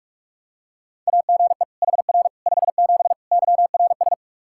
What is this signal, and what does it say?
Morse code, heard as a keyed tone, about 46 wpm, 700 Hz. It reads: AGE HR 57 YRS
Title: Practice QSO - Exchange 3 - More Info - 45wpm